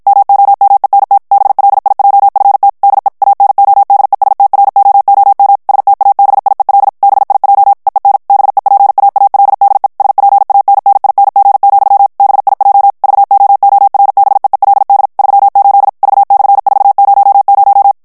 Morse code, heard as a keyed tone, about 52 wpm, 800 Hz. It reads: MOMENT BLIJFT DE ANODESTROOM STABIEL BIJ EEN BEPAALDE SPANNING, BIJ VOORBEELD 38 V/400
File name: HOLLANDAIS texte clair 2 vitesse 50 mots minute